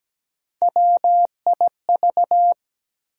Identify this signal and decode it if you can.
Morse code, heard as a keyed tone, about 17 wpm, 700 Hz. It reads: WIV
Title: USA Call Sign Postfixes - 3 Letters 17wpm